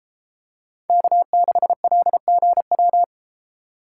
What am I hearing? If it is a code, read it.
Morse code, heard as a keyed tone, about 33 wpm, 700 Hz. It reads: K6LGW